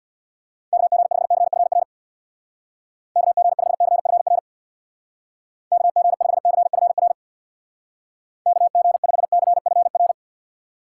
Morse code, heard as a keyed tone, about 64 wpm, 700 Hz. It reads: KG5CFR KG5CFR KG5CFR KG5CFR